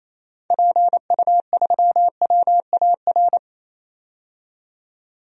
Morse code, heard as a keyed tone, about 28 wpm, 700 Hz. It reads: PU3WAR
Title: DXCC-100 Call Signs - 6 Characters - 28wpm